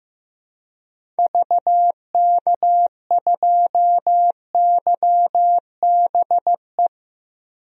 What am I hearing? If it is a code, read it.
Morse code, heard as a keyed tone, about 15 wpm, 700 Hz. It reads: VK2YBE
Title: DXCC-100 Call Signs - 6 Characters - 15wpm